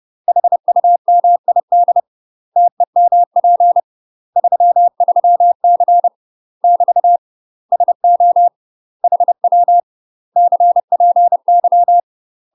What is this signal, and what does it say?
Morse code, heard as a keyed tone, about 30 wpm, 700 Hz. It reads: HUMID TEMP 33C <BT> SO HW CPY
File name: UK MASTER QSOs TRAINING / 30WPM / UK MASTER QSOs 0029